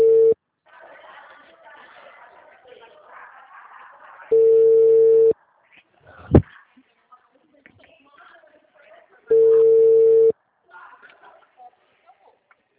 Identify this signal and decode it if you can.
Ringback tone (China, 450 Hz), with speech.
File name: out- -2013- - - .56249